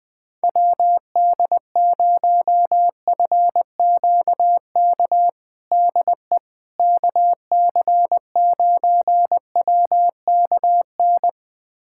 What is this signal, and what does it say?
Morse code, heard as a keyed tone, about 21 wpm, 700 Hz. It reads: WD0FQK DE KC9WKN
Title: USA Callsign DE Callsign 20wpm